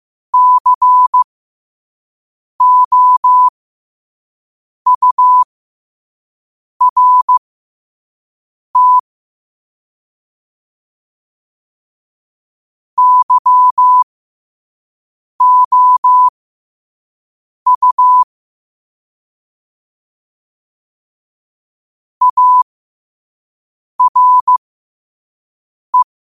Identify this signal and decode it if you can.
Morse code, heard as a keyed tone, about 15 wpm, spaced as for 5 wpm, 1000 Hz. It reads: COURT YOU ARE
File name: qotdfiles / 5wpm